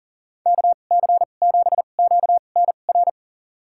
Morse code, heard as a keyed tone, about 40 wpm, spaced as for 29 wpm, 700 Hz. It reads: KC7QNR
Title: USA Call Signs - 2x Character Spacing - 40wpm